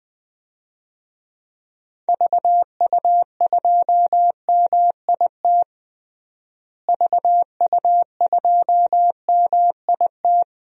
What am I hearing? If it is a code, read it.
Morse code, heard as a keyed tone, about 20 wpm, 700 Hz. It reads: VU2MIT VU2MIT
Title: DXCC-100 Call Signs - 6 Characters - Repeated 3x - 20wpm